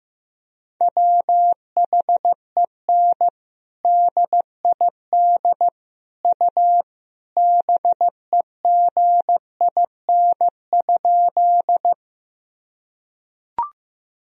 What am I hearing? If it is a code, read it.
Morse code, heard as a keyed tone, about 15 wpm, 700 Hz. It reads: WHEN DID U BEGIN?
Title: Sentences from Top 300 Words and Abbreviations 15wpm